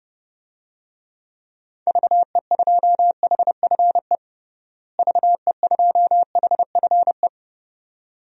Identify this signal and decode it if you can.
Morse code, heard as a keyed tone, about 30 wpm, 700 Hz. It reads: VE2HFE VE2HFE
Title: DXCC-100 Call Signs - 6 Characters - Repeated 2x - 30wpm